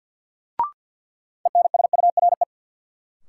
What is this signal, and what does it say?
Morse code, heard as a keyed tone, about 50 wpm, 700 Hz. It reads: ENSURE
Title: Sentences from Top 1100 Words - New Word Drill - 50wpm